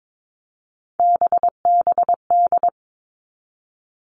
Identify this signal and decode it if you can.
Morse code, heard as a keyed tone, about 22 wpm, 700 Hz. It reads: BBD
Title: All Cut Numbers - 3-Digit 22wpm